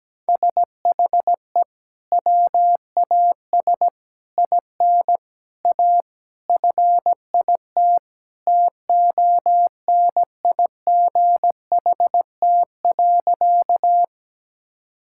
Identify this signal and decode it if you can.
Morse code, heard as a keyed tone, about 17 wpm, 700 Hz. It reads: SHE WAS IN A FIT TONIGHT.
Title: Sentences from Top 1000 Words - Review Part 5 - 17wpm